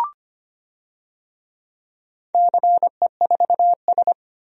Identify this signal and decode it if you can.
Morse code, heard as a keyed tone, about 25 wpm, 690 Hz. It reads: CE4S
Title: DXCC-100 Call Signs - 4 Characters - 25wpm